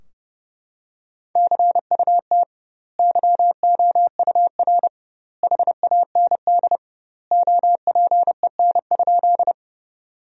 Morse code, heard as a keyed tone, about 30 wpm, 700 Hz. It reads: CUT YOUR HAND OPEN?
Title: 2x Word-Spacing - Sentences from Top 300 Words - 30wpm